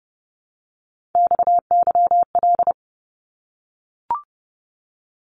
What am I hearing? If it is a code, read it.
Morse code, heard as a keyed tone, about 29 wpm, 700 Hz. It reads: XYL
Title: Abbreviations Repeated 3x - 30wpm